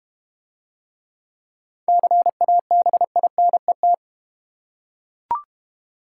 Morse code, heard as a keyed tone, about 32 wpm, 700 Hz. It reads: CABINET